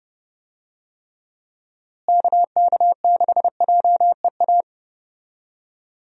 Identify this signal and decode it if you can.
Morse code, heard as a keyed tone, about 32 wpm, 700 Hz. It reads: KK6JEA
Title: USA Call Signs - 6 Characters - Repeated 2x - 30wpm